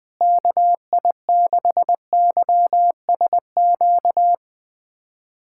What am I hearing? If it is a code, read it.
Morse code, heard as a keyed tone, about 20 wpm, 700 Hz. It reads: KI6YSQ
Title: USA Callsign DE Callsign - 6 Characters - 20wpm